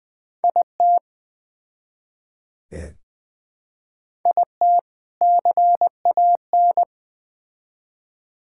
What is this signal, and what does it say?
Morse code, heard as a keyed tone, about 20 wpm, 700 Hz. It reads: IT IT CAN
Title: Build A Phrase - 3-Word Phrases - 20wpm